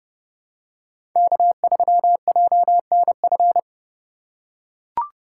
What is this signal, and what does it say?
Morse code, heard as a keyed tone, about 30 wpm, 700 Hz. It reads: K3JNF
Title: USA Call Signs - 5 Characters - 2x Character Spacing - 30wpm